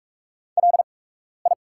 Morse code, heard as a keyed tone, about 44 wpm, spaced as for 12 wpm, 690 Hz. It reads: LI
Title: 8x Character Spacing - Top 200 Words - 45wpm